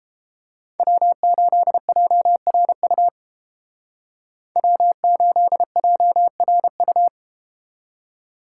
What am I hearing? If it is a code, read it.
Morse code, heard as a keyed tone, about 30 wpm, 700 Hz. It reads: W8JRU W8JRU